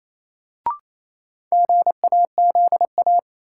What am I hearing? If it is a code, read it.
Morse code, heard as a keyed tone, about 28 wpm, 700 Hz. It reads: GAZA
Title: Sets of 5 Words - New Word Drill - 28wpm